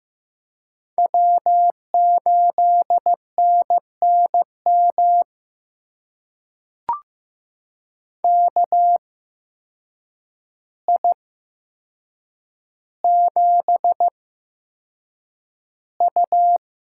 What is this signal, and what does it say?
Morse code, heard as a keyed tone, about 15 wpm, spaced as for 7 wpm, 700 Hz. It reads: W8NNM KI7U
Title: USA Call Signs - 5 Characters - 8x Character Spacing - 15wpm